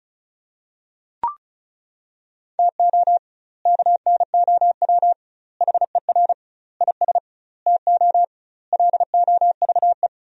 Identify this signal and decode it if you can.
Morse code, heard as a keyed tone, about 36 wpm, 700 Hz. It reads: TO KNOW HER IS TO LOVE
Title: 2x Word-Spacing - Sentences from Top 400 Words - 35wpm